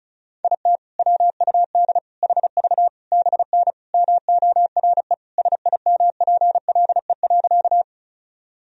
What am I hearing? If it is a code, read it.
Morse code, heard as a keyed tone, about 34 wpm, 700 Hz. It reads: IT WUD HV BN MORE SIMPLE.
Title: Sentences from Top 400 Words and Abbreviations 35wpm